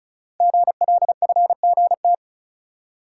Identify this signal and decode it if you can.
Morse code, heard as a keyed tone, about 36 wpm, 700 Hz. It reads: GLFGT
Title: Code Groups - 5 Letters - 35wpm